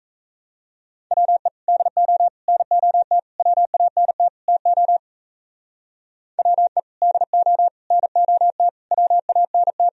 Morse code, heard as a keyed tone, about 35 wpm, 700 Hz. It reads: WE DO NOT WANT TO WE DO NOT WANT
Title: Speed-Racing - Sets of 5 Words 28wpm